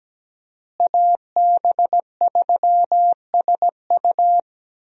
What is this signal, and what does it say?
Morse code, heard as a keyed tone, about 17 wpm, 700 Hz. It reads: AB3SU
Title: USA Call Signs - 5 Characters - Repeated 3x - 17wpm